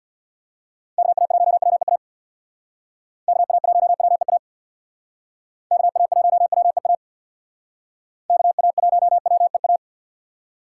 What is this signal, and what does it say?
Morse code, heard as a keyed tone, about 62 wpm, 700 Hz. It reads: KA1WEA KA1WEA KA1WEA KA1WEA
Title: Speed Racing - USA 2x3 Call Signs 50wpm